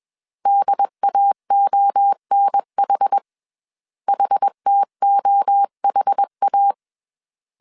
Morse code, heard as a keyed tone, about 21 wpm, 800 Hz. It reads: DAONH HTOHA